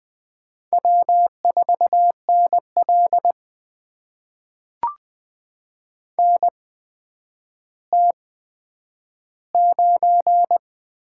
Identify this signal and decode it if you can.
Morse code, heard as a keyed tone, about 20 wpm, spaced as for 10 wpm, 700 Hz. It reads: W4NL NT9
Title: USA Call Signs - 4 Characters - 8x Character Spacing - 20wpm